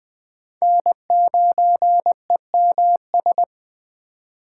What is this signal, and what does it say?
Morse code, heard as a keyed tone, about 20 wpm, 700 Hz. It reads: N9EMS
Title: USA Call Signs - 5 Characters - 8x Character Spacing - 20wpm